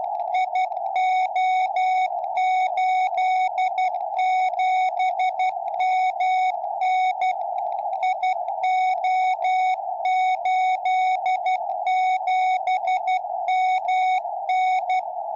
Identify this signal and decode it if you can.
Morse code, heard as a keyed tone, about 12 wpm, 740 Hz, with radio noise. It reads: IO87MN IO87MN
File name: tropo / gm4wja